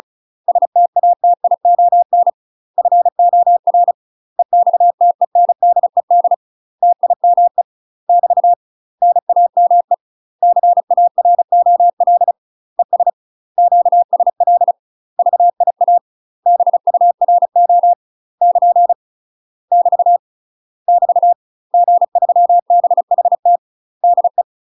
Morse code, heard as a keyed tone, about 35 wpm, 700 Hz. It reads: STATION FOR EXTENDED TIME <BT> NAME CAROL ES QSL VIA BURO <KN> <BT> <BT> G3BHT DE
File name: UK MASTER QSOs TRAINING / 35WPM / UK MASTER QSOs 0018